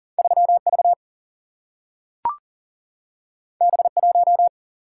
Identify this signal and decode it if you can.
Morse code, heard as a keyed tone, about 39 wpm, 700 Hz. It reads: 3 V B 1
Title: Single-Letter Single-Number 40wpm